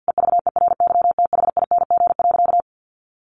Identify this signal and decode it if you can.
Morse code, heard as a keyed tone, about 50 wpm, 700 Hz. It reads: EVERYTHING.
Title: Sentences from Top 900 Words 50wpm